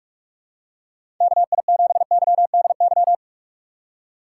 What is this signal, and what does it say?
Morse code, heard as a keyed tone, about 46 wpm, 700 Hz. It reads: KI7YDY
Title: USA Call Signs - 6 Characters - 3x Character Spacing - 45wpm